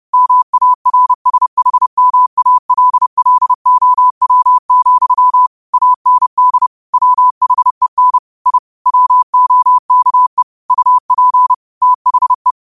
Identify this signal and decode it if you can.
Morse code, heard as a keyed tone, about 30 wpm, 1000 Hz. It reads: MARSHMALLOW, AND WHEN I WOKE UP THE